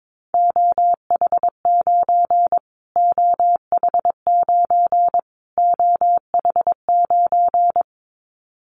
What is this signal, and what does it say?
Morse code, heard as a keyed tone, about 22 wpm, 700 Hz. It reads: OH9 OH9 OH9
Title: Callsign Prefixes for Top 25 DXCC - Repeated 3x - 22wpm